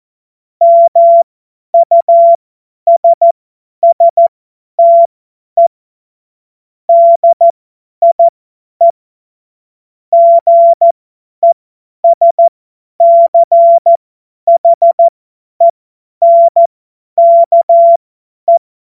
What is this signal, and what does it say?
Morse code, heard as a keyed tone, about 14 wpm, spaced as for 10 wpm, 680 Hz. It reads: MUSSTE DIE GESCHENKE
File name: Morsekurs / Witze / Witz 18. April 2023 50 F70